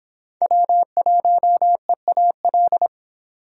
Morse code, heard as a keyed tone, about 30 wpm, 700 Hz. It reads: W1EAL